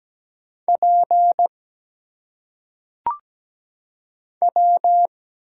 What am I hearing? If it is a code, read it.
Morse code, heard as a keyed tone, about 17 wpm, 700 Hz. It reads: P W
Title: Single Letters - Phonetic Pronunciation 17wpm